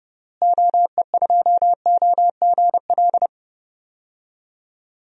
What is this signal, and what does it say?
Morse code, heard as a keyed tone, about 30 wpm, 700 Hz. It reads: OE2OGL
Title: DXCC-100 Call Signs - 6 Characters - Repeated 2x - 30wpm